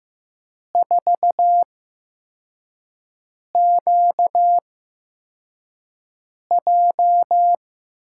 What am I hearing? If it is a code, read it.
Morse code, heard as a keyed tone, about 15 wpm, spaced as for 4 wpm, 700 Hz. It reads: 4QJ
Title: USA Call Signs - 8x Character Spacing - 15wpm